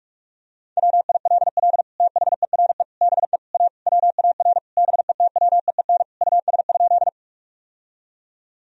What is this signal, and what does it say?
Morse code, heard as a keyed tone, about 46 wpm, 700 Hz. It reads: WILL THERE BE A WAR BETWEEN US?
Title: Sentences from Top 1000 Words - Review Part 2 - 45wpm